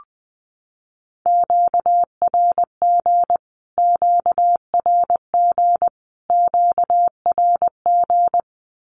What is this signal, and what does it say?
Morse code, heard as a keyed tone, about 20 wpm, 700 Hz. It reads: QRG QRG QRG
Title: Q-Codes Repeated 3x 20wpm